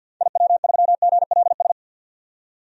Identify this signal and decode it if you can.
Morse code, heard as a keyed tone, about 50 wpm, 700 Hz. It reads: IW3GLS
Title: DXCC-100 Call Signs - 6 Characters - 50wpm